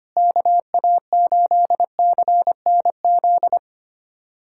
Morse code, heard as a keyed tone, about 25 wpm, 700 Hz. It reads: KA8CNZ
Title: USA Callsign DE Callsign 25wpm